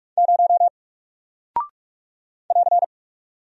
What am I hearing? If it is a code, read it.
Morse code, heard as a keyed tone, about 44 wpm, 700 Hz. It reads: 0 <AR>
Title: Single Character (Letter:Number:Punctuation:Prosign) Alt 45wpm